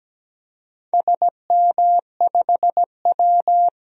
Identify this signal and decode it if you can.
Morse code, heard as a keyed tone, about 17 wpm, 700 Hz. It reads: SM5W
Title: DXCC-100 Call Signs - 4 Characters - 17wpm